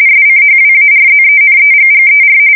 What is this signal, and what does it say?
RTTY, about 45.45 baud. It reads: SCHWARZENEGGER
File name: fsk170 / battleshipnj